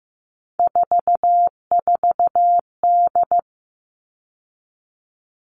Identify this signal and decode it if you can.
Morse code, heard as a keyed tone, about 15 wpm, 700 Hz. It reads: 44D
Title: All Cut Numbers - 3-Digit 15wpm